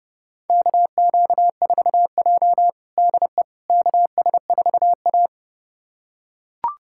Morse code, heard as a keyed tone, about 32 wpm, 700 Hz. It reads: KQ4J DE KS4A
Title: USA Callsign DE Callsign - 4 Characters - 30wpm